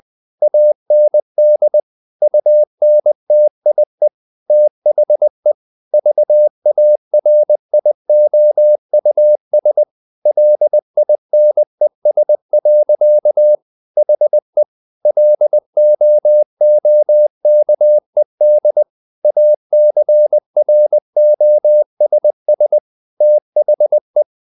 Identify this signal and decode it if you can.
Morse code, heard as a keyed tone, about 20 wpm, 600 Hz. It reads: AND UNTIE THE VARIOUS LINES. HE LOOKED ACROSS THE
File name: the old man and the sea / the old man and the sea part 1 20wpm